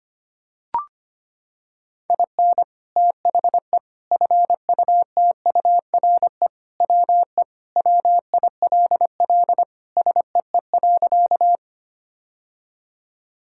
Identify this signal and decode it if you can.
Morse code, heard as a keyed tone, about 25 wpm, 700 Hz. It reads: IN THE FUTURE WE WILL SEE.
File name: practice / 25wpm / Sentences with 3-Word Phrases - 25wpm